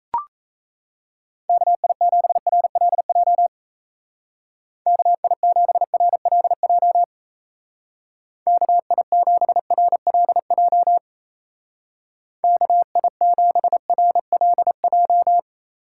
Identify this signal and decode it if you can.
Morse code, heard as a keyed tone, about 34 wpm, 700 Hz. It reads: KI7RLJ KI7RLJ KI7RLJ KI7RLJ
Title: Speed Racing - USA 2x3 Call Signs 28wpm